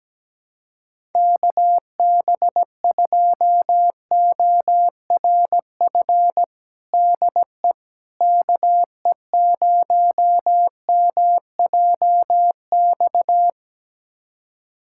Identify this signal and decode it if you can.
Morse code, heard as a keyed tone, about 17 wpm, 700 Hz. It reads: KB2ORF DE KE0MJX
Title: USA Callsign DE Callsign - 6 Characters - 17wpm